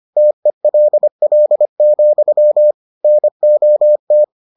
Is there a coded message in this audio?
Yes: Morse code from 0.2 to 4.2 s, heard as a keyed tone, about 25 wpm, 600 Hz. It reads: TELL, NOT